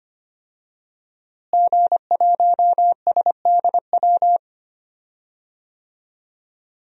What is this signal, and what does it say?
Morse code, heard as a keyed tone, about 25 wpm, 700 Hz. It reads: G1SDW
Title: DXCC-100 Call Signs - 5 Characters - Repeated 3x - 25wpm